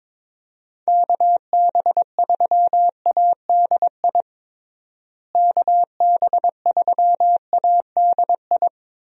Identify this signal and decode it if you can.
Morse code, heard as a keyed tone, about 22 wpm, 700 Hz. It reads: KB3ADI KB3ADI
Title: USA Call Signs - 6 Characters - Repeated 2x - 22wpm